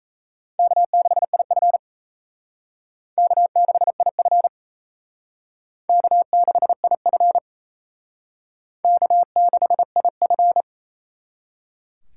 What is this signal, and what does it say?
Morse code, heard as a keyed tone, about 35 wpm, 700 Hz. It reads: K6IF K6IF K6IF K6IF
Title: Speed Racing - USA Call Signs 28wpm